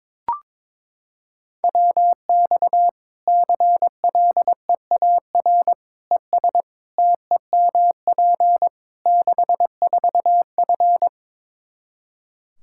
Morse code, heard as a keyed tone, about 22 wpm, 700 Hz. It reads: WX CLEAR ES TEMP 64F
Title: K1USN SST QSO Practice 22wpm